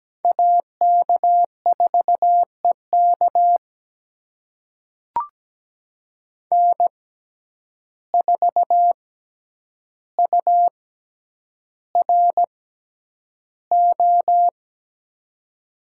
Morse code, heard as a keyed tone, about 17 wpm, spaced as for 9 wpm, 700 Hz. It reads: AK4EK N4URO
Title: USA Call Signs - 5 Characters - 6x Character Spacing - 17wpm